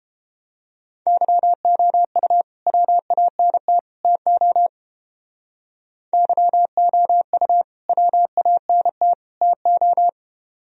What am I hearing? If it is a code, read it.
Morse code, heard as a keyed tone, about 29 wpm, 700 Hz. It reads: YOU WANT TO YOU WANT TO